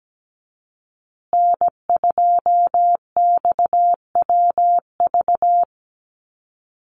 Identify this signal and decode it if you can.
Morse code, heard as a keyed tone, about 17 wpm, 700 Hz. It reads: N2XWV
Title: USA Call Signs Repeated 3x 17wpm